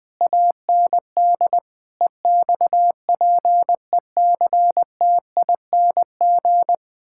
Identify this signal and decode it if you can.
Morse code, heard as a keyed tone, about 20 wpm, 700 Hz. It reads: AND EXPECTING
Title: Sentences with 9-Word Phrases - 20wpm